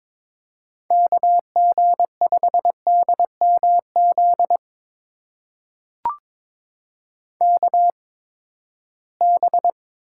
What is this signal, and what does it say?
Morse code, heard as a keyed tone, about 23 wpm, spaced as for 15 wpm, 700 Hz. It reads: KG5DMZ KB
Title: USA Call Signs - 6 Characters - 8x Character Spacing - 22wpm